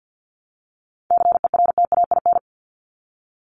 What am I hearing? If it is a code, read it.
Morse code, heard as a keyed tone, about 50 wpm, 700 Hz. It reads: CERTAIN
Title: LY Suffix - Top 100 Words with Base Words 50wpm